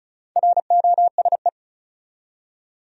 Morse code, heard as a keyed tone, about 36 wpm, 700 Hz. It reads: ROSE
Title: USA Common Names - Max 4 Letters - 35wpm